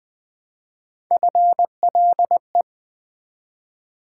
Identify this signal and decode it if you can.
Morse code, heard as a keyed tone, about 20 wpm, 700 Hz. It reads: FLE
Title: Build A Word - 1 Letter At A Time - 3 to 5-Letter Words - 20wpm